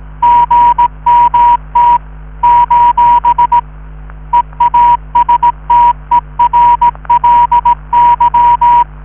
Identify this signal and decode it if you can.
Morse code, heard as a keyed tone, about 18 wpm, 960 Hz. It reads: GMT : EASTERLY